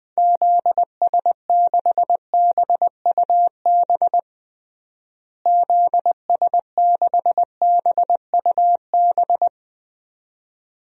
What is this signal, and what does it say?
Morse code, heard as a keyed tone, about 20 wpm, 700 Hz. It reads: ZS6BUB ZS6BUB
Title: DXCC-100 Call Signs - 6 Characters - Repeated 2x - 20wpm